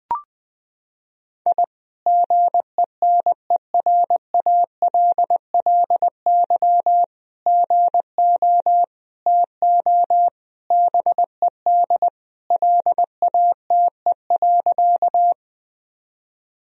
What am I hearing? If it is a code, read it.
Morse code, heard as a keyed tone, about 20 wpm, 700 Hz. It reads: I GENERALLY GO TO BED LATE.
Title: Sentences from Top 1000 Words - Review Part 4 - 20wpm